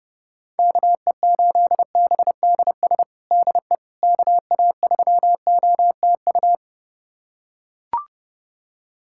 Morse code, heard as a keyed tone, about 30 wpm, 700 Hz. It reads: KE8BDS DE KA3OTU
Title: USA Callsign DE Callsign 30wpm